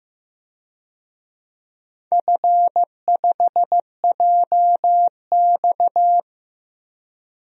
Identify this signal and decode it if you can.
Morse code, heard as a keyed tone, about 16 wpm, 700 Hz. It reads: F5JX
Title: DXCC-100 Call Signs - 4 Characters - 15wpm